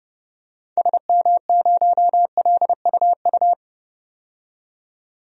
Morse code, heard as a keyed tone, about 30 wpm, 700 Hz. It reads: SM0LUU
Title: DXCC-100 Call Signs - 6 Characters - 30wpm